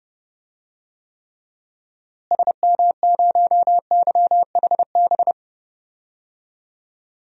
Morse code, heard as a keyed tone, about 30 wpm, 700 Hz. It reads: SM0YHB
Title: DXCC-100 Call Signs - 6 Characters - Repeated 2x - 30wpm